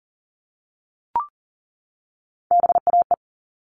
Morse code, heard as a keyed tone, about 40 wpm, 700 Hz. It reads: BAE
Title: All Cut Numbers - 3-Digit 40wpm